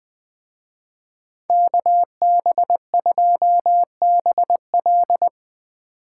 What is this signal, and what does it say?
Morse code, heard as a keyed tone, about 20 wpm, 700 Hz. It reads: KB2BL
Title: Speed Racing - USA Call Signs 17wpm